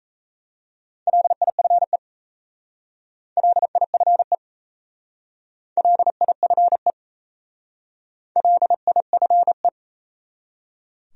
Morse code, heard as a keyed tone, about 34 wpm, 700 Hz. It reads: LIFE LIFE LIFE LIFE
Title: Speed-Racing Top 300 Words 28wpm